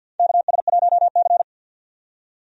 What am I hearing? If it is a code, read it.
Morse code, heard as a keyed tone, about 49 wpm, 700 Hz. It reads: KS1C
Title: USA Call Signs - 4 Characters - 50wpm